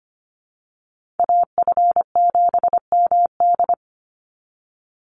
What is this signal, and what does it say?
Morse code, heard as a keyed tone, about 25 wpm, 700 Hz. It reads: AF7MD
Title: NA SOTA Chaser Callsigns 25wpm